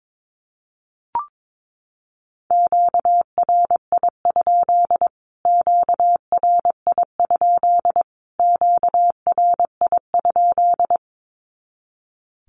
Morse code, heard as a keyed tone, about 22 wpm, 700 Hz. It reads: QRI? QRI? QRI?
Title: Q-Codes Repeated 3x 22wpm